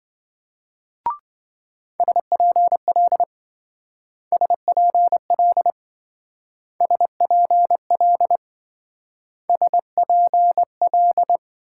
Morse code, heard as a keyed tone, about 26 wpm, 700 Hz. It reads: SPL SPL SPL SPL